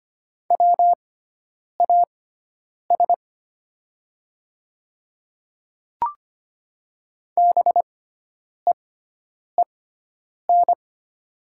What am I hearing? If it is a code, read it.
Morse code, heard as a keyed tone, about 25 wpm, spaced as for 9 wpm, 700 Hz. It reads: WAS BEEN
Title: 6x Character Spacing - Top 300 Words - 25wpm